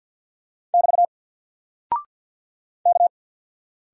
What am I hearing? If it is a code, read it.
Morse code, heard as a keyed tone, about 48 wpm, 700 Hz. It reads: <BT> K
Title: Single Character (Letter:Number:Punctuation:Prosign) 50wpm